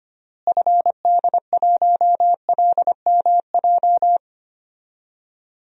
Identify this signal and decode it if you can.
Morse code, heard as a keyed tone, about 25 wpm, 700 Hz. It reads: FD1LMJ
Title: DXCC-100 Call Signs - 6 Characters - Repeated 2x - 25wpm